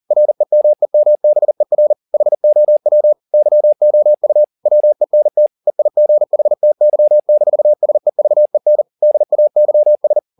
Morse code, heard as a keyed tone, about 40 wpm, 600 Hz. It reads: REMEMBER HOW YOU WENT EIGHTY-SEVEN DAYS